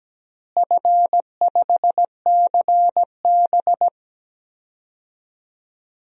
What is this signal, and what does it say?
Morse code, heard as a keyed tone, about 17 wpm, 700 Hz. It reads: F5CB
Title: DXCC-100 Call Signs - 4 Characters - Repeated 2x - 17wpm